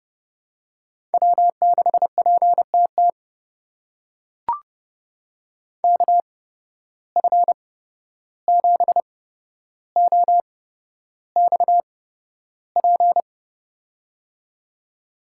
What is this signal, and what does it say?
Morse code, heard as a keyed tone, about 32 wpm, spaced as for 12 wpm, 700 Hz. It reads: W6PTT KF7OXP
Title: USA Call Signs - 8x Character Spacing - 30wpm